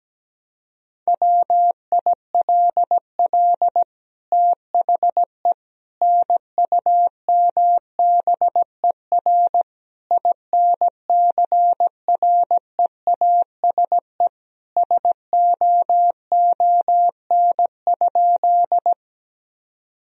Morse code, heard as a keyed tone, about 17 wpm, 700 Hz. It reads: WILL THE NUMBER INCREASE SOON?
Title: Sentences from Top 1000 Words - Review Part 4 - 17wpm